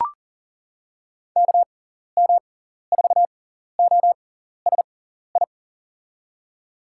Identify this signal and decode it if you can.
Morse code, heard as a keyed tone, about 40 wpm, spaced as for 14 wpm, 700 Hz. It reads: KM4OSI
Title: USA Call Signs - 6 Characters - 6x Character Spacing - 40wpm